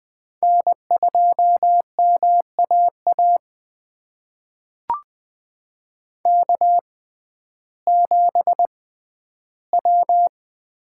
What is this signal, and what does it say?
Morse code, heard as a keyed tone, about 20 wpm, spaced as for 12 wpm, 700 Hz. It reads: N2MAA K7W
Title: USA Call Signs - 5 Characters - 6x Character Spacing - 20wpm